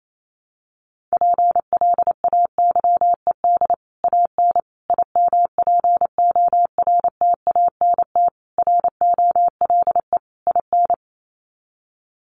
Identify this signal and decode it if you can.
Morse code, heard as a keyed tone, about 28 wpm, 700 Hz. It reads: PLAYED AN IMPORTANT ROLE IN
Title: Sets of 5 Words Encore2 28wpm